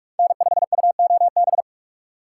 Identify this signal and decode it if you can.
Morse code, heard as a keyed tone, about 39 wpm, 700 Hz. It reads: N5UOB